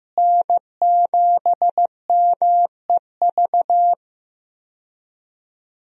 Morse code, heard as a keyed tone, about 15 wpm, 700 Hz. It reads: N7MEV
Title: USA Call Signs - 5 Characters - Repeated 2x - 15wpm